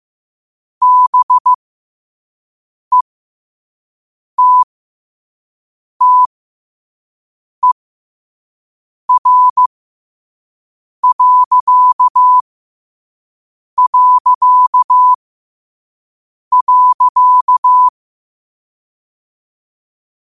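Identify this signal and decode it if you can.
Morse code, heard as a keyed tone, about 15 wpm, spaced as for 5 wpm, 1000 Hz. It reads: BETTER...
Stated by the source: Quote-of-the-day Morse code practice at 5 wpm.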